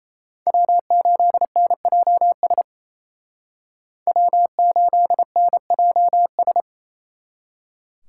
Morse code, heard as a keyed tone, about 34 wpm, 700 Hz. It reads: W8NJS W8NJS
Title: Speed Racing - USA 1x3 Call Signs 28wpm